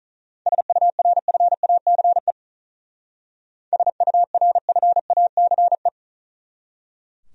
Morse code, heard as a keyed tone, about 44 wpm, 700 Hz. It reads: SURFACE SURFACE